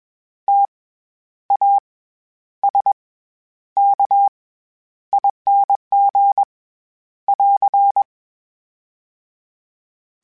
Morse code, heard as a keyed tone, about 21 wpm, 800 Hz. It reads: T A S K ING <AR>
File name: Special INGs-mp3s / Special INGs-mp3s / SlowToFaster ing @20